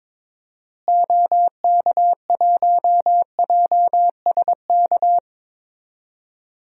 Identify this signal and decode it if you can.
Morse code, heard as a keyed tone, about 22 wpm, 700 Hz. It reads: OK1JSK
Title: DXCC-100 Call Signs - 6 Characters - Repeated 2x - 22wpm